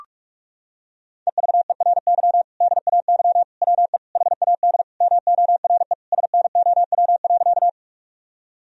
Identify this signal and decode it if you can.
Morse code, heard as a keyed tone, about 41 wpm, 700 Hz. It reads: EVERY DAY WE HAD MORE SNOW.
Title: Sentences from Top 1000 Words - Review Part 3 - 45wpm